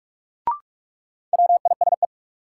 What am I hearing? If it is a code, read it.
Morse code, heard as a keyed tone, about 41 wpm, 700 Hz. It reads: WISE
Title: Sentences from Top 1600 Words - New Word Drill - 45wpm